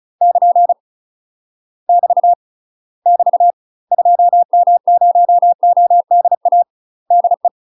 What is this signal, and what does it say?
Morse code, heard as a keyed tone, about 34 wpm, 700 Hz. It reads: <KN> <BT> <BT> 2M0ODA DE
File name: UK MASTER QSOs TRAINING / 35WPM / UK MASTER QSOs 0039